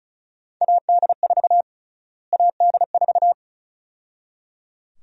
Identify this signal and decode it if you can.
Morse code, heard as a keyed tone, about 34 wpm, 700 Hz. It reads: AD4 AD4
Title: USA Call Sign Prefixes - Repeated 2x - 35wpm